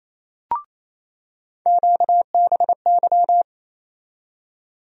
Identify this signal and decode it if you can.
Morse code, heard as a keyed tone, about 28 wpm, 700 Hz. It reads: QBY
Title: USA Call Sign Postfixes - 3 Letters 28wpm